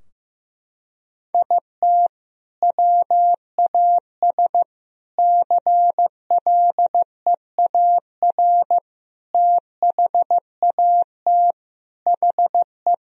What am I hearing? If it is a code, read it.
Morse code, heard as a keyed tone, about 16 wpm, 700 Hz. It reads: IT WAS CLEAR THAT HE
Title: Sentences from Top 1000 Words - Review Part 4 - 15wpm